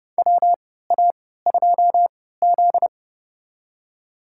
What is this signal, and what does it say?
Morse code, heard as a keyed tone, about 30 wpm, spaced as for 17 wpm, 700 Hz. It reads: WA2Z
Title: USA Call Signs - 4 Characters - 3x Character Spacing - 30wpm